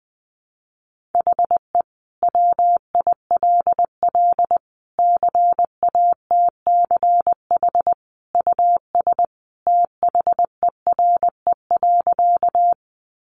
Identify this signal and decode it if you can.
Morse code, heard as a keyed tone, about 20 wpm, 700 Hz. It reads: HE WILL CATCH US THERE.
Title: Sentences from Top 1500 Words 20wpm